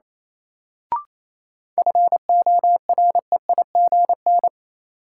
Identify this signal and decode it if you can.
Morse code, heard as a keyed tone, about 28 wpm, 700 Hz. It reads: FOREIGN
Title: Sentences from Top 1400 Words - New Word Drill - 28wpm